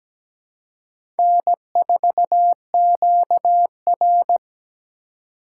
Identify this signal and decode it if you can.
Morse code, heard as a keyed tone, about 17 wpm, 700 Hz. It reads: N4QR
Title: USA Call Signs - 4 Characters - 2x Character Spacing - 17wpm